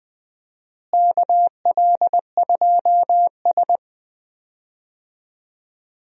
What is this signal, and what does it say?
Morse code, heard as a keyed tone, about 20 wpm, 700 Hz. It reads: KL2S
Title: USA Call Signs - 4 Characters - Repeated 2x - 20wpm